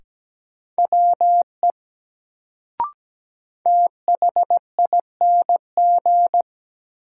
Morse code, heard as a keyed tone, about 17 wpm, 700 Hz. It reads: WE THING
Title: Sentences from Top 100 Words - New Word Drill - 17wpm